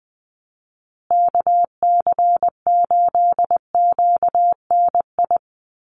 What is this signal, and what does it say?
Morse code, heard as a keyed tone, about 20 wpm, 700 Hz. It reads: KC8QNI
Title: USA Call Signs Repeated 2x 20wpm